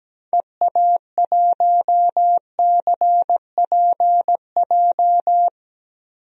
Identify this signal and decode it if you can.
Morse code, heard as a keyed tone, about 17 wpm, 700 Hz. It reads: EA1CPJ
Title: DXCC-100 Call Signs - 6 Characters - 17wpm